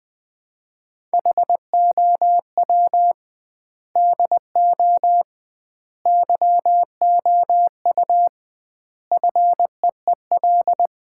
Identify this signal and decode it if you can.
Morse code, heard as a keyed tone, about 20 wpm, 700 Hz. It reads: HOW DO YOU FEEL
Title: 2x Word-Spacing - Sets of 5 Words - 20wpm